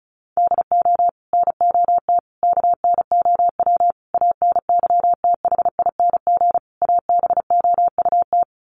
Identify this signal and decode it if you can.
Morse code, heard as a keyed tone, about 34 wpm, 700 Hz. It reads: DO NOT KNOW ANYTHING ABOUT
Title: Sets of 5 Words 35wpm